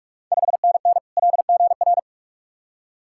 Morse code, heard as a keyed tone, about 45 wpm, 700 Hz. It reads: 5NN LGR
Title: Ontario QSO Party Exchange 45wpm